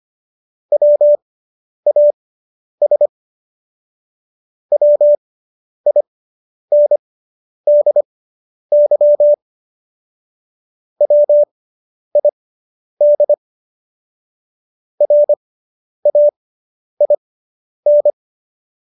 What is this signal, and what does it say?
Morse code, heard as a keyed tone, about 25 wpm, spaced as for 10 wpm, 590 Hz. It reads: WAS WINDY WID RAIN